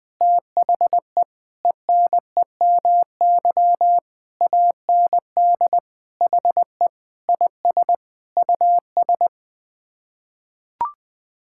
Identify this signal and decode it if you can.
Morse code, heard as a keyed tone, about 21 wpm, 700 Hz. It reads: THE ENEMY AND HE IS US
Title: Sets of 9 Words - 20wpm